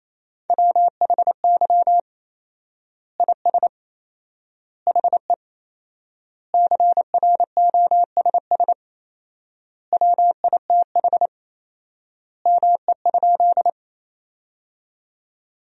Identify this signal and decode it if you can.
Morse code, heard as a keyed tone, about 27 wpm, 700 Hz. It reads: WHY IS HE CROSS WITH ME?
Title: 4x Word-Spacing - Sentences from Top 300 Words - 28wpm